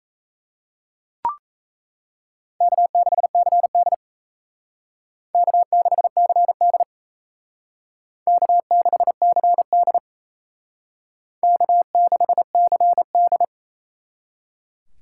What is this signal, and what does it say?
Morse code, heard as a keyed tone, about 35 wpm, 700 Hz. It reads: K6CD K6CD K6CD K6CD
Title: Speed Racing - USA 1x2 Call Signs 28wpm